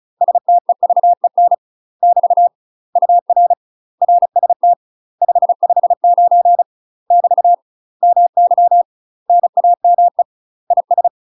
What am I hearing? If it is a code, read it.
Morse code, heard as a keyed tone, about 36 wpm, 700 Hz. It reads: STEVEN <BT> UR RST 559 <BT> MY NAME IS
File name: UK MASTER QSOs TRAINING / 35WPM / UK MASTER QSOs 0035